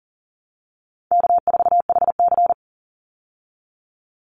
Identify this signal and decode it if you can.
Morse code, heard as a keyed tone, about 40 wpm, 700 Hz. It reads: K4HC
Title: Easy Call Signs Repeated 3x - 40wpm